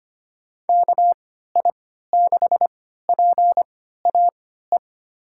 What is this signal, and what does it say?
Morse code, heard as a keyed tone, about 25 wpm, spaced as for 14 wpm, 700 Hz. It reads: KI6PAE
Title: USA Call Signs - 6 Characters - 3x Character Spacing - 25wpm